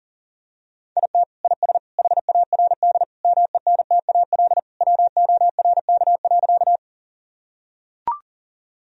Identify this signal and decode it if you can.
Morse code, heard as a keyed tone, about 40 wpm, 700 Hz. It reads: IT IS HARD MENTAL WORK.
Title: Sentences from Top 1000 Words - Review Part 4 - 40wpm